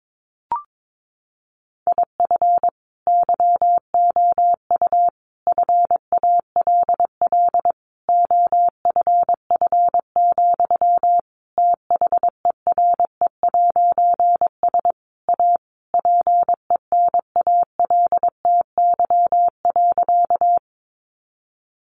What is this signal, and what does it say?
Morse code, heard as a keyed tone, about 22 wpm, 700 Hz. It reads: IF YOU FALL OFF, THERE'S A PENALTY.
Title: A Sound Of Thunder 22wpm